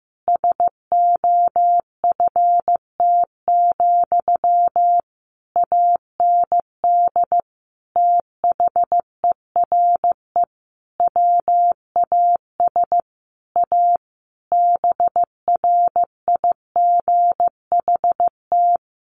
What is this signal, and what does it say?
Morse code, heard as a keyed tone, about 15 wpm, 700 Hz. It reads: SOFT, AND THERE WAS A BRIGHT
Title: Sentences from Top 1500 Words 15wpm